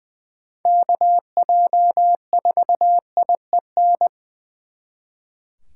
Morse code, heard as a keyed tone, about 24 wpm, 700 Hz. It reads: KJ4IEN
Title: Speed Racing - USA 2x3 Call Signs 20wpm